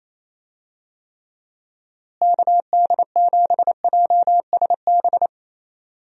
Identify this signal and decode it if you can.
Morse code, heard as a keyed tone, about 28 wpm, 700 Hz. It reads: KD7JSB
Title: USA Call Signs - 6 Characters - Repeated 3x - 28wpm